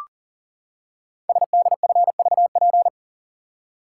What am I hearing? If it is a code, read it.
Morse code, heard as a keyed tone, about 38 wpm, 700 Hz. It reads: SDFVP
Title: Code Groups - 5 Letter:Numbers - 40wpm